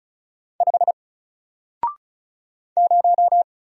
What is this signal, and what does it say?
Morse code, heard as a keyed tone, about 36 wpm, 700 Hz. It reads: 5 0
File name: Single-Letter-Number-35wpm